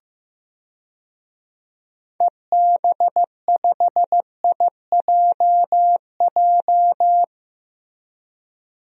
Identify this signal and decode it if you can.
Morse code, heard as a keyed tone, about 15 wpm, 700 Hz. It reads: EB5IJJ
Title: DXCC-100 Call Signs - 6 Characters - Repeated 2x - 15wpm